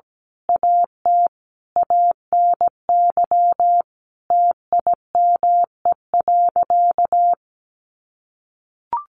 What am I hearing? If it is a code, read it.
Morse code, heard as a keyed tone, about 17 wpm, 700 Hz. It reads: AT ANY TIME.
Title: Sentences from Top 1500 Words 17wpm